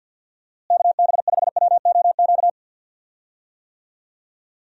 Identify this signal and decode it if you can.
Morse code, heard as a keyed tone, about 50 wpm, 700 Hz. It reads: KB5WOQ
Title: USA Call Signs - 6 Characters - Repeated 3x - 50wpm